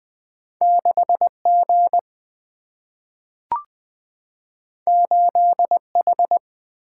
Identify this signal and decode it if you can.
Morse code, heard as a keyed tone, about 20 wpm, 700 Hz. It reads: 6G 8H
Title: Single-Letter Single-Number 20wpm